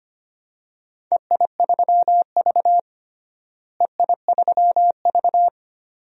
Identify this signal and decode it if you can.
Morse code, heard as a keyed tone, about 25 wpm, 700 Hz. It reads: EI3V EI3V
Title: DXCC-100 Call Signs - 4 Characters - Repeated 2x - 25wpm